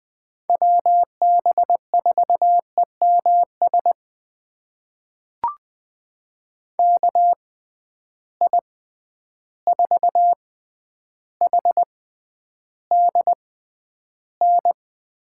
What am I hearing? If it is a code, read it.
Morse code, heard as a keyed tone, about 20 wpm, spaced as for 10 wpm, 700 Hz. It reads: WB4EMS KI4HDN
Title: USA Call Signs - 6 Characters - 6x Character Spacing - 20wpm